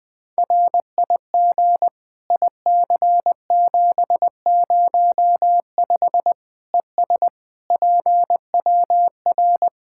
Morse code, heard as a keyed tone, about 20 wpm, 700 Hz. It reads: RIG IC705 ES PWR
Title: Practice QSO - Exchange 2 - Info - 20wpm